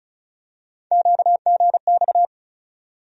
Morse code, heard as a keyed tone, about 34 wpm, 700 Hz. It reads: QGX
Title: USA Call Sign Postfixes - 3 Letters 35wpm